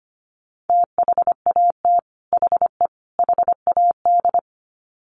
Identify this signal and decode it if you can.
Morse code, heard as a keyed tone, about 25 wpm, 700 Hz. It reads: THAT HE HAD
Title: Sets of 3 Words 25wpm